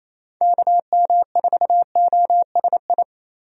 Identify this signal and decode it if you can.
Morse code, heard as a keyed tone, about 28 wpm, 700 Hz. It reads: KM4OSI
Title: USA Call Signs - 6 Characters - 28wpm